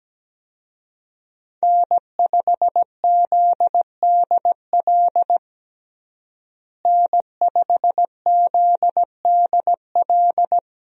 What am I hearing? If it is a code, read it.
Morse code, heard as a keyed tone, about 17 wpm, 700 Hz. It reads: N5ZDL N5ZDL
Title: USA Call Signs - 5 Characters - Repeated 3x - 17wpm